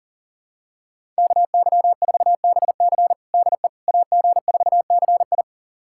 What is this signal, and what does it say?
Morse code, heard as a keyed tone, about 40 wpm, 700 Hz. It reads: KY4BC DE AG4CI
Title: USA Callsign DE Callsign - 5 Characters - 40wpm